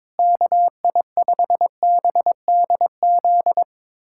Morse code, heard as a keyed tone, about 22 wpm, 700 Hz. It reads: KI5BDZ
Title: USA Callsign DE Callsign - 6 Characters - 22wpm